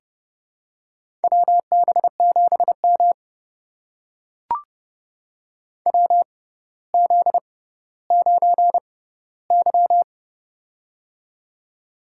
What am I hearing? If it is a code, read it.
Morse code, heard as a keyed tone, about 30 wpm, spaced as for 15 wpm, 700 Hz. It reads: WB7M WZ9Y
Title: USA Call Signs - 4 Characters - 6x Character Spacing - 30wpm